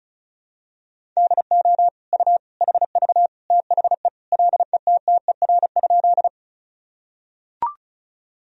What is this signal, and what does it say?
Morse code, heard as a keyed tone, about 36 wpm, 700 Hz. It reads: DO U HV THE LETTER?
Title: Sentences from Top 300 Words and Abbreviations 35wpm